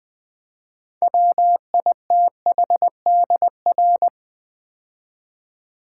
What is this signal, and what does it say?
Morse code, heard as a keyed tone, about 20 wpm, 700 Hz. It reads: WITHDR
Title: Build A Word - 2 Letters At A Time - 8-Letter Words - 20wpm